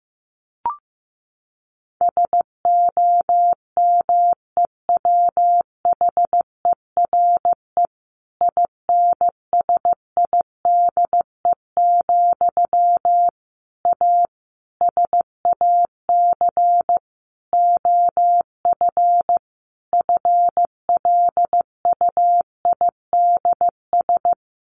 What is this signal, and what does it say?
Morse code, heard as a keyed tone, about 15 wpm, 700 Hz. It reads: SOMEWHERE INSIDE, A SAC OF FLUIDS
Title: A Sound Of Thunder 15wpm